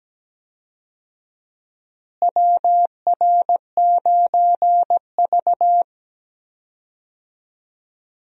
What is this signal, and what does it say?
Morse code, heard as a keyed tone, about 17 wpm, 700 Hz. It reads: WR9V
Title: USA Call Signs - 4 Characters - Repeated 3x - 17wpm